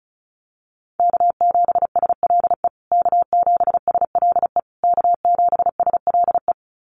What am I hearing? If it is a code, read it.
Morse code, heard as a keyed tone, about 34 wpm, 700 Hz. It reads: K7SLE K7SLE K7SLE
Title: USA Call Signs Repeated 3x 35wpm